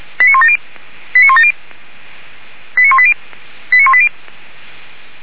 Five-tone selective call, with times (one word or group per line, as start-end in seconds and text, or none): 0.2-0.6 89170
1.1-1.5 89170
2.8-3.1 89180
3.7-4.1 89180